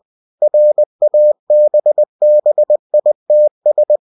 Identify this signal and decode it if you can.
Morse code, heard as a keyed tone, about 20 wpm, 600 Hz. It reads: RABBITS
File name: cw-books / winnie-the-pooh / winnie-20wpm-page-0005